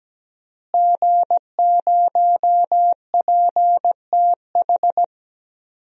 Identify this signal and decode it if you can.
Morse code, heard as a keyed tone, about 17 wpm, 700 Hz. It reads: G0PTH
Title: DXCC-100 Call Signs - 5 Characters - Repeated 2x - 17wpm